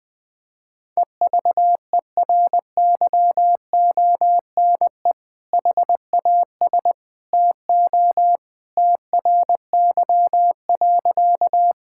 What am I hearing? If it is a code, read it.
Morse code, heard as a keyed tone, about 21 wpm, 700 Hz. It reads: EVERYONE HAS TO TRY.
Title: Sentences with 2-Word Phrases - Spoken Beginning - 20wpm